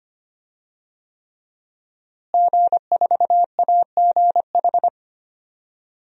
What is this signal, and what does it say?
Morse code, heard as a keyed tone, about 25 wpm, 700 Hz. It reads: G4AGH
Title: DXCC-100 Call Signs - 5 Characters - Repeated 3x - 25wpm